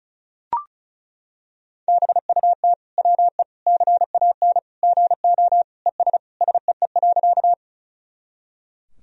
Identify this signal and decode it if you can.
Morse code, heard as a keyed tone, about 34 wpm, 700 Hz. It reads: BUT WE CAN GO ES SEE.
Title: Sentences from Top 100 Words and Abbreviations 35wpm